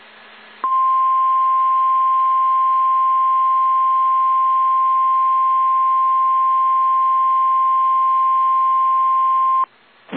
Emergency Alert System (SAME) tones (the 1050 Hz attention tone). It is heard from 0.6 to 9.7 s.
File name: eas / KSDZKDJL / KSDZKDJL